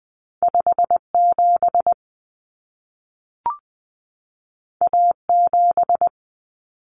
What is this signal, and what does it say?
Morse code, heard as a keyed tone, about 20 wpm, 700 Hz. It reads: 57 A7
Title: Common Cut Numbers - 2-Digit 20wpm